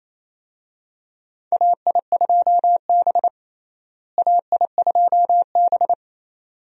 Morse code, heard as a keyed tone, about 28 wpm, 700 Hz. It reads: AI2B AI2B
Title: USA Call Signs - 4 Characters - Repeated 2x - 28wpm